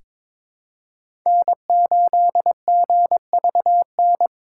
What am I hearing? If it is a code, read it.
Morse code, heard as a keyed tone, about 22 wpm, 700 Hz. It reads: N8GVN
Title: USA Call Signs - 5 Characters - 3x Character Spacing - 22wpm